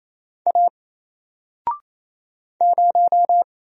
Morse code, heard as a keyed tone, about 27 wpm, 700 Hz. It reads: A 0
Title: Single Character (Letter:Number:Punctuation:Prosign) 28wpm